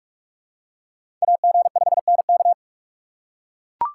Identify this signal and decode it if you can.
Morse code, heard as a keyed tone, about 44 wpm, 700 Hz. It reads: AG5NK